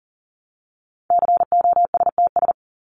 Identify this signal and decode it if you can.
Morse code, heard as a keyed tone, about 40 wpm, 700 Hz. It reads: COSTS
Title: ES:S Suffix - Top 100 Words with Base Words 40wpm